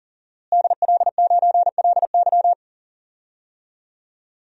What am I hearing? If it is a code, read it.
Morse code, heard as a keyed tone, about 40 wpm, 700 Hz. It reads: DL9LY
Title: DXCC-100 Call Signs - 5 Characters - Repeated 2x - 40wpm